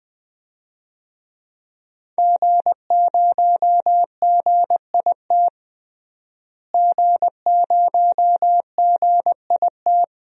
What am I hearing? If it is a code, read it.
Morse code, heard as a keyed tone, about 20 wpm, 700 Hz. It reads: G0GIT G0GIT
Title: DXCC-100 Call Signs - 5 Characters - Repeated 2x - 20wpm